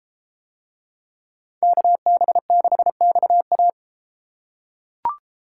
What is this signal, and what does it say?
Morse code, heard as a keyed tone, about 32 wpm, 700 Hz. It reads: KB6XA